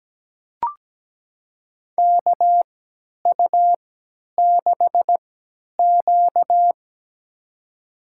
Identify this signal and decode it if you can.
Morse code, heard as a keyed tone, about 17 wpm, spaced as for 10 wpm, 700 Hz. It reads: KU6Q
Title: USA Call Signs - 4 Characters - 3x Character Spacing - 17wpm